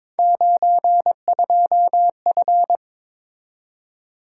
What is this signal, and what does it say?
Morse code, heard as a keyed tone, about 23 wpm, 700 Hz. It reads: 92F
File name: practice / 22wpm / Three Letter-Number Combinations - 22wpm